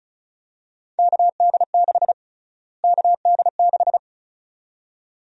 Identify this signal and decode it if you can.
Morse code, heard as a keyed tone, about 36 wpm, 700 Hz. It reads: KD6 KD6
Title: USA Call Sign Prefixes - Repeated 2x - 35wpm